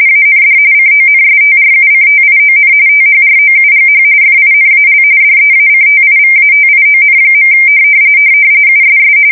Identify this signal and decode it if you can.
RTTY, about 45.45 baud. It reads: AND CONSTRUCTION THROUGH FINAL DEACTIVATION. USS
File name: fsk170 / battleshipnj